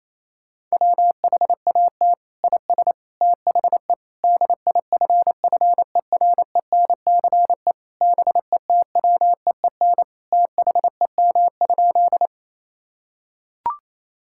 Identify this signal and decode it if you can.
Morse code, heard as a keyed tone, about 27 wpm, 700 Hz. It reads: WHAT IS THE DIFFERENCE BETWEEN THEM?
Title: Sentences from Top 1000 Words - Review Part 5 - 28wpm